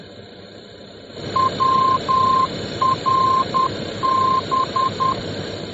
Morse code, heard as a keyed tone, about 10 wpm, 1050 Hz, with noise. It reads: WRB